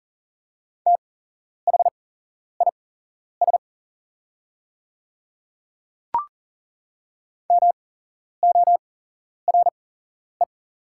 Morse code, heard as a keyed tone, about 40 wpm, spaced as for 11 wpm, 700 Hz. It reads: THIS MORE
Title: 8x Character Spacing - Top 100 Words - 40wpm